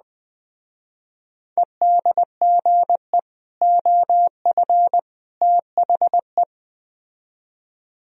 Morse code, heard as a keyed tone, about 20 wpm, 700 Hz. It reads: EDGE OF THE